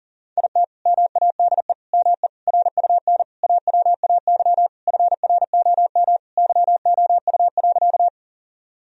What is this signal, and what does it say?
Morse code, heard as a keyed tone, about 40 wpm, 700 Hz. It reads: IT MADE ME RUN AWAY FROM YOU.
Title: Sentences from Top 1000 Words - Review Part 1 - 40wpm